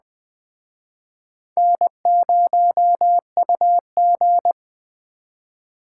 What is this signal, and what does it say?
Morse code, heard as a keyed tone, about 21 wpm, 700 Hz. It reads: N0UG
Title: Speed Racing - USA 1x2 Call Signs 17wpm